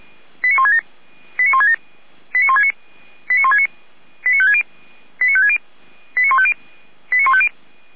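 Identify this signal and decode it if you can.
Five-tone selective call (ZVEI), heard as quick runs of five tones, tones of about 70 ms. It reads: 89167 89167 89169 89169 89560 89560 89150 89150
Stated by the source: Five-tone ZVEI selective call received on SDR, from feuerwehr-orlamuende.org fire brigade.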